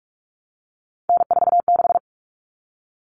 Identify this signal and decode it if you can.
Morse code, heard as a keyed tone, about 44 wpm, 700 Hz. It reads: N46
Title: Common Cut Numbers - 3-Digit 45wpm